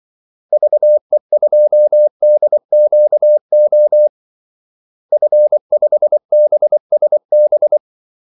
Morse code, heard as a keyed tone, about 24 wpm, 600 Hz. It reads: VE2DQO F5BSB
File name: Callsigns-24wpm